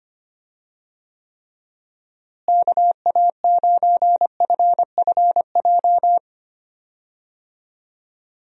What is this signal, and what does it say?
Morse code, heard as a keyed tone, about 25 wpm, 700 Hz. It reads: KA9FFJ
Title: USA Call Signs - 6 Characters - Repeated 3x - 25wpm